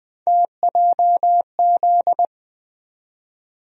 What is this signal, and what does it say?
Morse code, heard as a keyed tone, about 20 wpm, 700 Hz. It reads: TJZ
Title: USA Call Sign Postfixes - 3 Letters 20wpm